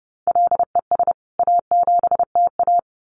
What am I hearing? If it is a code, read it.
Morse code, heard as a keyed tone, about 30 wpm, 700 Hz. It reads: LES A7TA
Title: CWT Exchange 30wpm